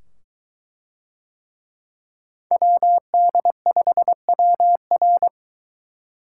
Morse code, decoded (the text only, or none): WD5WR